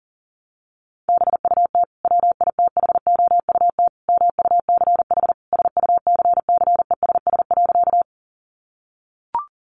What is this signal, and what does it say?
Morse code, heard as a keyed tone, about 40 wpm, 700 Hz. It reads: BUT WITHOUT MUCH SUCCESS.
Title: Sentences from Top 900 Words 40wpm